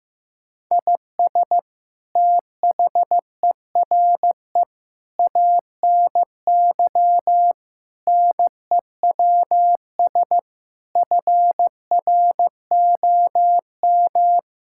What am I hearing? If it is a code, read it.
Morse code, heard as a keyed tone, about 15 wpm, 700 Hz. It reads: IS THERE ANY NEWS FROM
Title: Sentences from Top 1000 Words - Review Part 5 - 15wpm